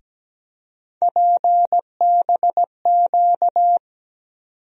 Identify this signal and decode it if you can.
Morse code, heard as a keyed tone, about 17 wpm, 700 Hz. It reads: PBQ
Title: USA Call Sign Postfixes - 3 Letters 17wpm